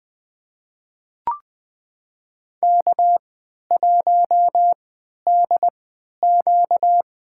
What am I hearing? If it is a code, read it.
Morse code, heard as a keyed tone, about 21 wpm, spaced as for 11 wpm, 700 Hz. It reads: K1DQ
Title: USA Call Signs - 4 Characters - 3x Character Spacing - 20wpm